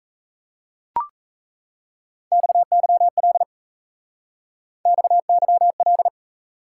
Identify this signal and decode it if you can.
Morse code, heard as a keyed tone, about 34 wpm, 700 Hz. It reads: XYL XYL
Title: Speed-Racing Top 180 QSO Elements 28wpm